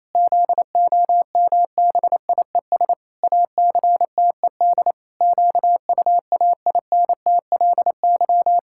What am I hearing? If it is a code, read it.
Morse code, heard as a keyed tone, about 28 wpm, 700 Hz. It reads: ZOMBIES ACTED QUAINTLY
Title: Pangram Challenge v2 - 28wpm